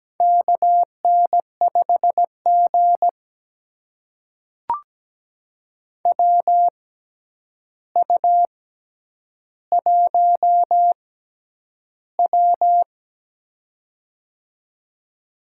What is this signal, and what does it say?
Morse code, heard as a keyed tone, about 17 wpm, spaced as for 9 wpm, 700 Hz. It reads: KN5G WU1W
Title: USA Call Signs - 4 Characters - 6x Character Spacing - 17wpm